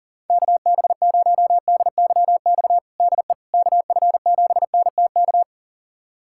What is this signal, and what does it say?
Morse code, heard as a keyed tone, about 40 wpm, 700 Hz. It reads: KB0DYX DE KF7NTK
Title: USA Callsign DE Callsign - 6 Characters - 40wpm